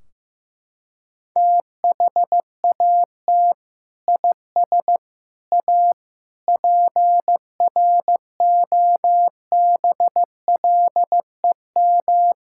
Morse code, heard as a keyed tone, about 15 wpm, 700 Hz. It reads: THAT IS A PROBLEM
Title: Sentences from Top 1600 Words 15wpm